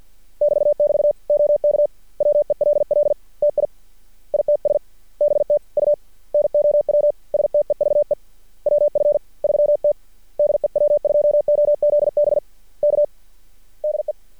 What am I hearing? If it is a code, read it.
Morse code, heard as a keyed tone, about 53 wpm, 590 Hz. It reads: <BT><BT> OK WELL TI ITS BTU NOW STEVE WF3T DE W2OZB K DE
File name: archived / CRSnet